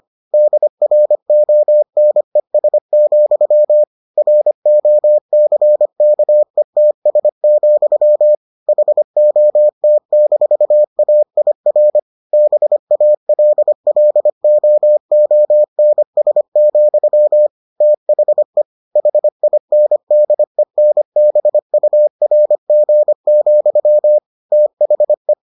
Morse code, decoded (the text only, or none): DRONES, ROCKETS, HOT-AIR BALLOONS, THE HINDENBURG, THE